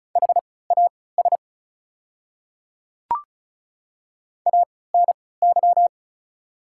Morse code, heard as a keyed tone, about 36 wpm, spaced as for 20 wpm, 700 Hz. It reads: HAS ANY
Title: 3x Character Spacing - Top 500 Words - 35wpm